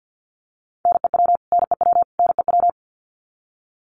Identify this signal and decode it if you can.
Morse code, heard as a keyed tone, about 48 wpm, 700 Hz. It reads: NEW NEW NEW
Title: Top 200 Words Repeated 50wpm